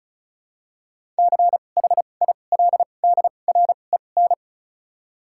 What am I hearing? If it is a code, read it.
Morse code, heard as a keyed tone, about 36 wpm, 700 Hz. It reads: C H I L D R E N
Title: 2x Character Spacing - Top 500 Words - 35wpm